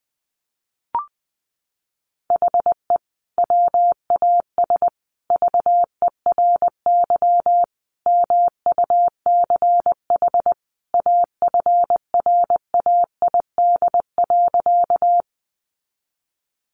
Morse code, heard as a keyed tone, about 20 wpm, 700 Hz. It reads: HE WAS VERY MUCH AFRAID.
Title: Sentences from Top 1500 Words 20wpm